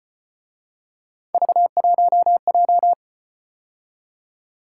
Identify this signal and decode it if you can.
Morse code, heard as a keyed tone, about 32 wpm, 700 Hz. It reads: V1J